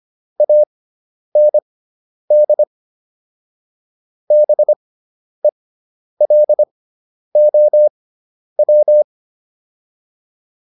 Morse code, heard as a keyed tone, about 25 wpm, spaced as for 10 wpm, 600 Hz. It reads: AND BELOW